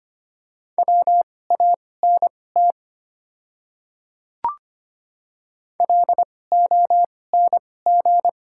Morse code, heard as a keyed tone, about 25 wpm, spaced as for 18 wpm, 700 Hz. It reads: WANT LONG
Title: 2x Character Spacing - Top 200 Words - 25wpm